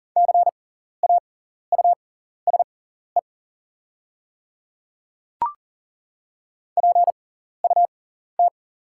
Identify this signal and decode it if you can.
Morse code, heard as a keyed tone, about 40 wpm, spaced as for 14 wpm, 690 Hz. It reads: CAUSE PUT
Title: 6x Character Spacing - Top 300 Words - 40wpm